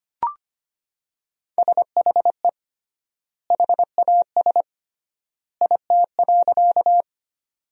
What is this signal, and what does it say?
Morse code, heard as a keyed tone, about 25 wpm, 700 Hz. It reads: SHE HAS IT.
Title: 3x Word-Spacing - Sentences from Top 100 Words - 25wpm